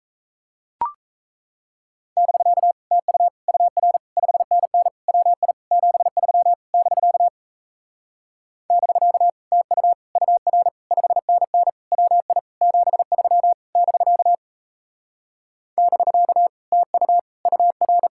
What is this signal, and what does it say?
Morse code, heard as a keyed tone, about 37 wpm, 700 Hz. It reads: <BK> TU UR 5NN WI 73 <BK> <BK> TU UR 5NN WI 73 <BK> <BK> TU UR